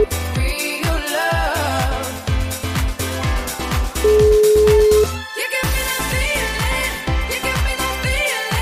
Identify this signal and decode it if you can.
Ringback tone (Europe, 425 Hz), with noise.